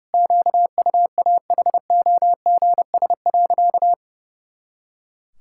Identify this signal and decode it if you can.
Morse code, heard as a keyed tone, about 29 wpm, 700 Hz. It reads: QUAHOGS.
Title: Pangram Challenge v2 - 30wpm